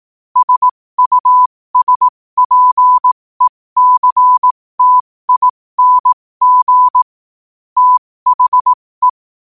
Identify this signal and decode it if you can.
Morse code, heard as a keyed tone, about 18 wpm, spaced as for 15 wpm, 1000 Hz. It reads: SUSPECTING THE
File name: qotdfiles / 15wpm